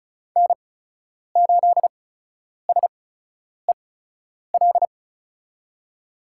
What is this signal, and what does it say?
Morse code, heard as a keyed tone, about 36 wpm, spaced as for 10 wpm, 700 Hz. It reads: N8SEL
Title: USA Call Signs - 5 Characters - 8x Character Spacing - 35wpm